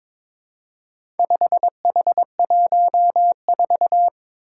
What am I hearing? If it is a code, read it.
Morse code, heard as a keyed tone, about 22 wpm, 700 Hz. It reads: 5H14
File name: practice / 22wpm / Four Letter-Number Combinations - 22wpm